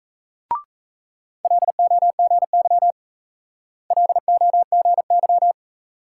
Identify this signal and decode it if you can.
Morse code, heard as a keyed tone, about 35 wpm, 700 Hz. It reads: LOGY LOGY
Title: Speed-Racing Multi-letter Phonemes - 28wpm